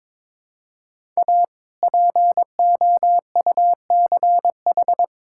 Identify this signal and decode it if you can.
Morse code, heard as a keyed tone, about 21 wpm, 700 Hz. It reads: A POUCH
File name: All-Age Appropriate Jokes - 22wpm